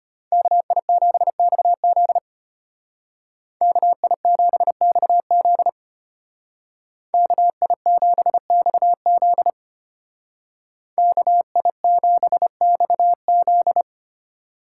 Morse code, heard as a keyed tone, about 33 wpm, 700 Hz. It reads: KI7XZ KI7XZ KI7XZ KI7XZ